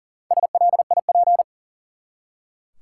Morse code, heard as a keyed tone, about 40 wpm, 700 Hz. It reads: SLIP
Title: Four Letter Words - 40wpm